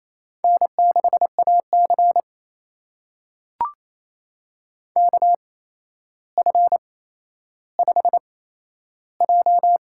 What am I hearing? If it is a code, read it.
Morse code, heard as a keyed tone, about 28 wpm, spaced as for 12 wpm, 700 Hz. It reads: N6AC KF5J
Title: USA Call Signs - 8x Character Spacing - 28wpm